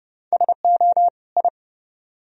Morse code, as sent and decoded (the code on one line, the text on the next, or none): ... --- / ..
SO I